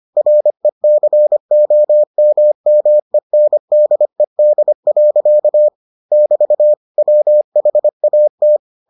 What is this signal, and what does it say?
Morse code, heard as a keyed tone, about 25 wpm, 600 Hz. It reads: RECOMMENDED. <BT> WHAT